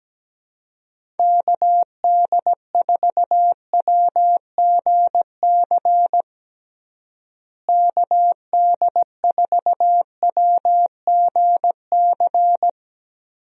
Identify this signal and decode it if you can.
Morse code, heard as a keyed tone, about 17 wpm, 700 Hz. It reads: KD4WGC KD4WGC
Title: USA Call Signs - 6 Characters - Repeated 3x - 17wpm